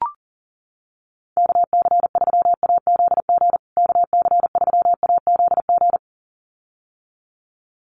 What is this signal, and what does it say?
Morse code, heard as a keyed tone, about 40 wpm, 700 Hz. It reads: KC3AZG KC3AZG
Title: USA Call Signs Repeated 2x 40wpm